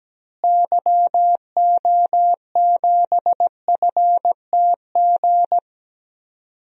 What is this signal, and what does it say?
Morse code, heard as a keyed tone, about 17 wpm, 700 Hz. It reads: YO7FTG
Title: DXCC-100 Call Signs - 6 Characters - Repeated 3x - 17wpm